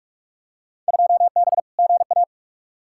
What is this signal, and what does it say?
Morse code, heard as a keyed tone, about 46 wpm, 700 Hz. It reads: 2B GA
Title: Field Day Exchanges Encore - 45wpm